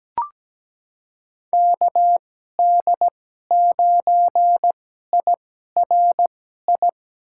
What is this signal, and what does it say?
Morse code, heard as a keyed tone, about 17 wpm, spaced as for 12 wpm, 700 Hz. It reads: KD9IRI
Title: USA Call Signs - 6 Characters - 2x Character Spacing - 17wpm